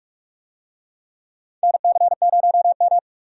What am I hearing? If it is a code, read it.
Morse code, heard as a keyed tone, about 44 wpm, 700 Hz. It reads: NC0M
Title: USA Call Signs - 4 Characters - 45wpm